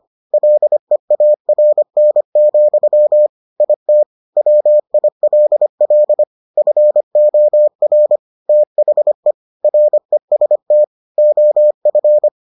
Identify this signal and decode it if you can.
Morse code, heard as a keyed tone, about 25 wpm, 600 Hz. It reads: LEARN, IT WILL FOR THE REST OF